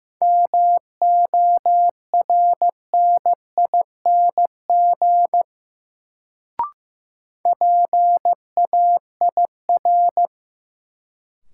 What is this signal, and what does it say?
Morse code, heard as a keyed tone, about 15 wpm, 700 Hz. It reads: MORNING PAIR
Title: Sets of 3 Words - New Word Drill - 15wpm